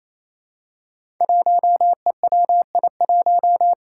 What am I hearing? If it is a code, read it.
Morse code, heard as a keyed tone, about 29 wpm, 700 Hz. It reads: 1EWI1
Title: Five Challenging Letter-Number Combinations 28wpm